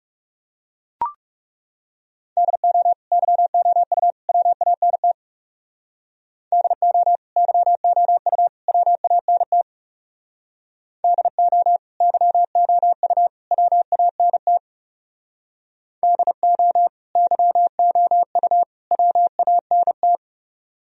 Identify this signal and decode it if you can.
Morse code, heard as a keyed tone, about 36 wpm, 700 Hz. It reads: DO YOU WANT DO YOU WANT DO YOU WANT DO YOU WANT
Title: Speed-Racing - Sets of 3 Words 30wpm